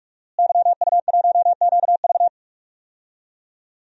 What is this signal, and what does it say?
Morse code, heard as a keyed tone, about 43 wpm, 700 Hz. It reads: YU1QV
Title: DXCC-100 Call Signs - 5 Characters - Repeated 2x - 45wpm